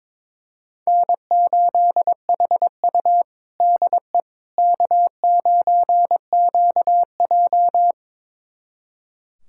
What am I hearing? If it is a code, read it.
Morse code, heard as a keyed tone, about 21 wpm, 700 Hz. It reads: N8HU DE K9QJ
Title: USA Callsign DE Callsign - 4 Characters - 22wpm